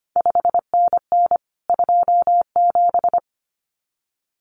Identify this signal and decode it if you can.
Morse code, heard as a keyed tone, about 25 wpm, 700 Hz. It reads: 5NN 27
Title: CQ WW DX Contest Exchange - 25wpm